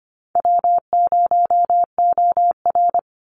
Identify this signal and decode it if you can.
Morse code, heard as a keyed tone, about 25 wpm, 700 Hz. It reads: W0OR
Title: Call Signs Repeated 3x - 25wpm